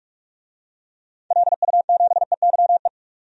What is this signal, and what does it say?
Morse code, heard as a keyed tone, about 46 wpm, 700 Hz. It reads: LU7EYE
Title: DXCC-100 Call Signs - 6 Characters - 45wpm